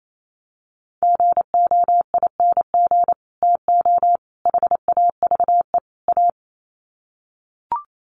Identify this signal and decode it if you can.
Morse code, heard as a keyed tone, about 28 wpm, 700 Hz. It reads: GOING TO HAVE A
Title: Sets of 4 Words 28wpm